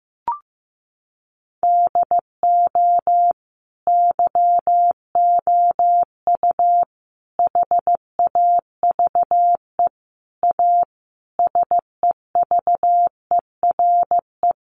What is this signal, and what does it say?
Morse code, heard as a keyed tone, about 15 wpm, 700 Hz. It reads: DO YOU HAVE A SEVERE
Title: Sentences from Top 1400 Words 15wpm